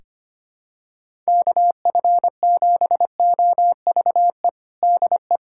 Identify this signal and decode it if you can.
Morse code, heard as a keyed tone, about 25 wpm, 700 Hz. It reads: KF7OVE DE
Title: USA Callsign DE Callsign - 6 Characters - 25wpm